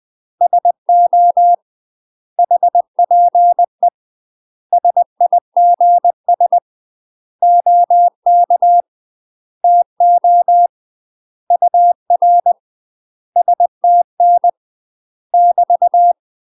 Morse code, heard as a keyed tone, about 20 wpm, 700 Hz. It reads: SO HPE SIGS OK TO UR STN <BT>
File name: UK MASTER QSOs TRAINING / 20WPM / UK MASTER QSOs 0026